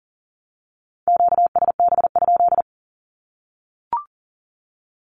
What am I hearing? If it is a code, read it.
Morse code, heard as a keyed tone, about 38 wpm, 700 Hz. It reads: QSB?
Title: Q-Codes Repeated 3x 40wpm